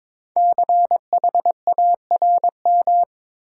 Morse code, heard as a keyed tone, about 22 wpm, 700 Hz. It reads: CHARM
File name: 5 Letter Words - First 3 Letters Given - 22wpm